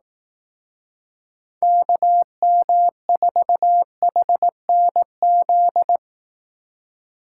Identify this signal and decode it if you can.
Morse code, heard as a keyed tone, about 18 wpm, 700 Hz. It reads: KM4HNZ
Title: Speed Racing - USA Call Signs 15wpm